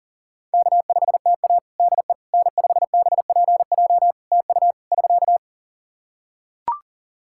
Morse code, heard as a keyed tone, about 40 wpm, 700 Hz. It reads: K5TA DE N5BPJ TU <SK>
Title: K1USN SST QSO Practice 40wpm